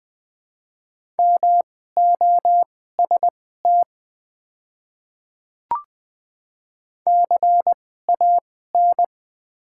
Morse code, heard as a keyed tone, about 20 wpm, spaced as for 14 wpm, 700 Hz. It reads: MOST CAN
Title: 2x Character Spacing - Top 100 Words - 20wpm